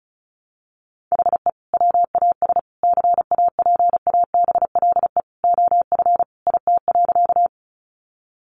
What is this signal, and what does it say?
Morse code, heard as a keyed tone, about 34 wpm, 700 Hz. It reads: HE WAS CAPABLE OF IT.
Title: Sentences from Top 1200 Words 35wpm